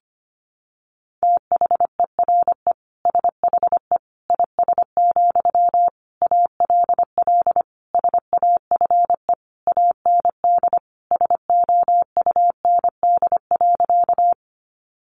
Morse code, heard as a keyed tone, about 25 wpm, 700 Hz. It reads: THERE SHE IS, ALL SAFE AND SOUND.
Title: Sentences from Top 900 Words 25wpm